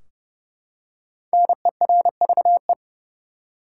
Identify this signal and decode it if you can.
Morse code, heard as a keyed tone, about 30 wpm, 700 Hz. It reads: NERVE
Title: Five Letter Words - 30wpm